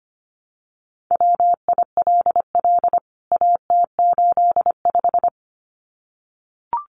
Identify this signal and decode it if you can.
Morse code, heard as a keyed tone, about 25 wpm, 700 Hz. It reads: WILL AT85
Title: CWT Exchange 25wpm